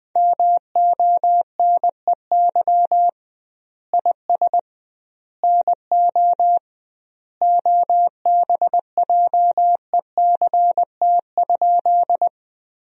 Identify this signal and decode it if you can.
Morse code, heard as a keyed tone, about 21 wpm, 700 Hz. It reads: MONEY IS NO OBJECT?
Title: 2x Word-Spacing - Sentences from Top 500 Words - 20wpm